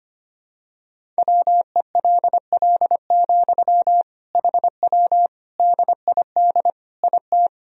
Morse code, heard as a keyed tone, about 25 wpm, 700 Hz. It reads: WELL, HW DID IT
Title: Sentences from Top 200 Words and Abbreviations 25wpm